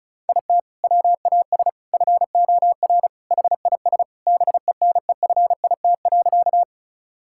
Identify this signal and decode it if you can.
Morse code, heard as a keyed tone, about 36 wpm, 700 Hz. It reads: IT WAS FOR HIS BENEFIT.
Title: Sentences from Top 1000 Words - Review Part 6 - 35wpm